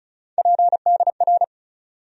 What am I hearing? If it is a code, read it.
Morse code, heard as a keyed tone, about 36 wpm, 700 Hz. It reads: PDR
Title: USA Call Sign Postfixes - 3 Letters 35wpm